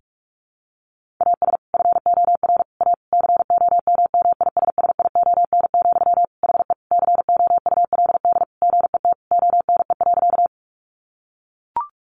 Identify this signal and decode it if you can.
Morse code, heard as a keyed tone, about 44 wpm, 700 Hz. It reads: AS FOR A COMMISSION, HE COULD GET ONE.
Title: Sentences from Top 1400 Words 45wpm